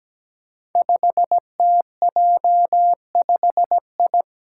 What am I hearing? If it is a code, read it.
Morse code, heard as a keyed tone, about 17 wpm, 700 Hz. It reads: 5TJ5I
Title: Five Challenging Letter-Number Combinations 17wpm